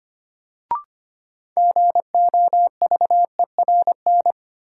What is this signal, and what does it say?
Morse code, heard as a keyed tone, about 25 wpm, 700 Hz. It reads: GOVERN
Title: Sentences from Top 1000 Words - Review Part 2 - New Word Drill - 25wpm